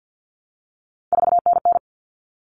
Morse code, heard as a keyed tone, about 50 wpm, 700 Hz. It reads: 4NN
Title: Common Cut Numbers - 3-Digit 50wpm